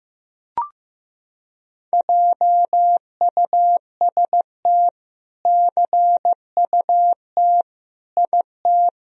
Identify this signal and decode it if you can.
Morse code, heard as a keyed tone, about 15 wpm, 700 Hz. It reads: JUST CUT IT
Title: Sentences from Top 1000 Words - Review Part 1 - 15wpm